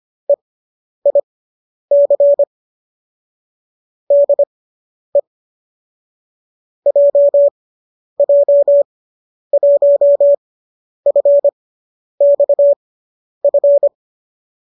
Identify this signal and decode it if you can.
Morse code, heard as a keyed tone, about 25 wpm, spaced as for 10 wpm, 570 Hz. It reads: EIC DE JJ1FXF
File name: qso005 10